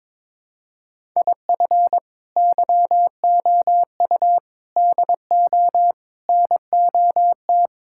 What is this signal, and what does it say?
Morse code, heard as a keyed tone, about 22 wpm, 700 Hz. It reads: IF YOU DO NOT
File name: practice / 22wpm / Build A Phrase - 5-Word Phrases - 22wpm